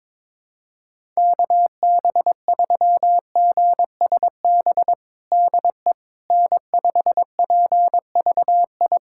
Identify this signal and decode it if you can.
Morse code, heard as a keyed tone, about 22 wpm, 700 Hz. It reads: KB3GSB DE N5PVI
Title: USA Callsign DE Callsign 22wpm